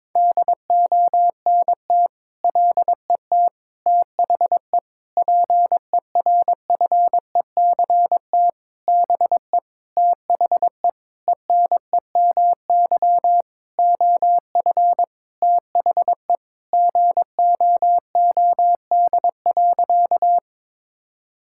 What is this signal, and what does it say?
Morse code, heard as a keyed tone, about 22 wpm, 700 Hz. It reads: DONT LET THE PERFECT BE THE ENEMY OF THE GOOD.
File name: Sentences with 9-Word Phrases - 22wpm